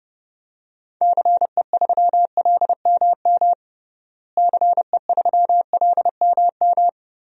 Morse code, heard as a keyed tone, about 30 wpm, 700 Hz. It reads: CE3LMM CE3LMM
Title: DXCC-100 Call Signs - 6 Characters - Repeated 3x - 30wpm